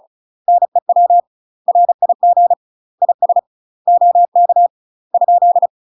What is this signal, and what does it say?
Morse code, heard as a keyed tone, about 33 wpm, 700 Hz. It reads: NEW RIG IS OK ?
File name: UK MASTER QSOs TRAINING / 35WPM / UK MASTER QSOs 0002